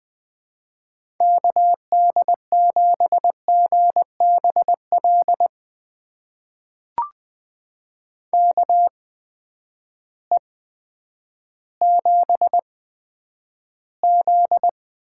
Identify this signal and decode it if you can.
Morse code, heard as a keyed tone, about 20 wpm, spaced as for 10 wpm, 700 Hz. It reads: KD7GBL KE7Z
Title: USA Call Signs - 6 Characters - 8x Character Spacing - 20wpm